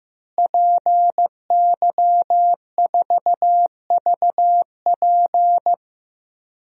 Morse code, heard as a keyed tone, about 15 wpm, 700 Hz. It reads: PY4VP
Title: DXCC-100 Call Signs - 5 Characters - Repeated 3x - 15wpm